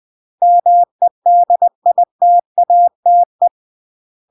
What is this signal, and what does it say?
Morse code, heard as a keyed tone, about 20 wpm, 700 Hz. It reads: MEDITATE
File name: CW / jodorowsky / 82 commandments 20-20